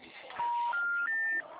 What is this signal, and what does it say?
Special information tone, with noise.